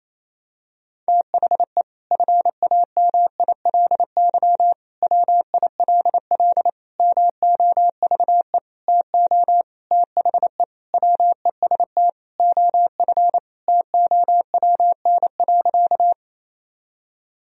Morse code, heard as a keyed tone, about 27 wpm, 700 Hz. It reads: THE FAMILY WILL MOVE TO THE WEST OF TOWN.
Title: Sentences from Top 1000 Words - Review Part 2 - 28wpm